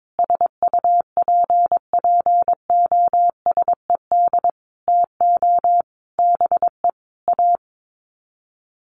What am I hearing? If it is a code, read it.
Morse code, heard as a keyed tone, about 22 wpm, 700 Hz. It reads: SUPPOSED TO BE A
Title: Sets of 5 Words Encore 22wpm